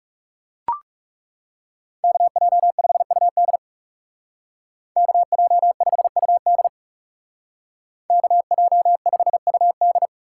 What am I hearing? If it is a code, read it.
Morse code, heard as a keyed tone, about 35 wpm, 700 Hz. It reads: KJ5UD KJ5UD KJ5UD
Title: Speed Racing - USA 2x2 Call Signs 30wpm